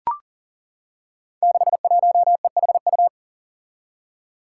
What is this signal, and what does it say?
Morse code, heard as a keyed tone, about 40 wpm, 700 Hz. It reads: 61EHU
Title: Five Challenging Letter-Number Combinations 40wpm